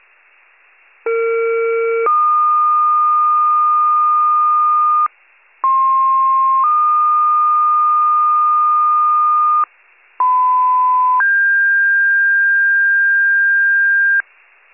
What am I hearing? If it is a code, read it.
Two-tone page, heard as a one-second tone, then a three-second tone, about 460 then 1220 Hz, about 1060 then 1220 Hz, about 1010 then 1600 Hz. It is heard from 1.1 to 14.2 s.
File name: wcalert